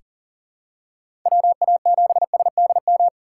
Morse code, heard as a keyed tone, about 40 wpm, 690 Hz. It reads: WA7SDM
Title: USA Call Signs - 4x Character Spacing - 40wpm